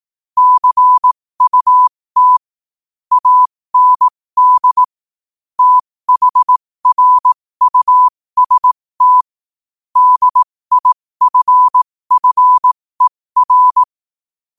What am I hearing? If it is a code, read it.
Morse code, heard as a keyed tone, about 18 wpm, spaced as for 15 wpm, 1000 Hz. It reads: CUT AND THRUST DIFFER
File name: qotdfiles / 15wpm